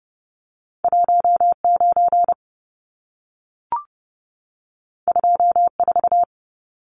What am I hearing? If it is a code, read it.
Morse code, heard as a keyed tone, about 30 wpm, 700 Hz. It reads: 19 24
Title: Numbers - 2-Digit 30wpm